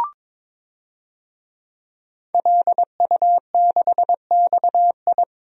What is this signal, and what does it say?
Morse code, heard as a keyed tone, about 22 wpm, 700 Hz. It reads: LU6XI
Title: DXCC-100 Call Signs - 5 Characters - 22wpm